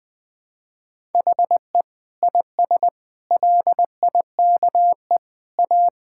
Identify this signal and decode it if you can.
Morse code, heard as a keyed tone, about 21 wpm, 700 Hz. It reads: HE IS LIKE A
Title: Sentences from Top 1000 Words - Review Part 5 - 20wpm